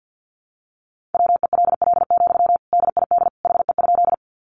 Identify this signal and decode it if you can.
Morse code, heard as a keyed tone, about 50 wpm, 700 Hz. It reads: WELL, DID HE?
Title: Sentences from Top 200 Words 50wpm